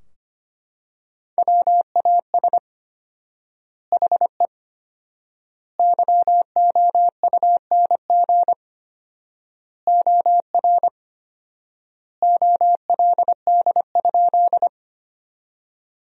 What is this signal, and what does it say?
Morse code, heard as a keyed tone, about 25 wpm, 700 Hz. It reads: WAS HE YOUNG OR OLD?
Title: 4x Word-Spacing - Sentences from Top 400 Words - 25wpm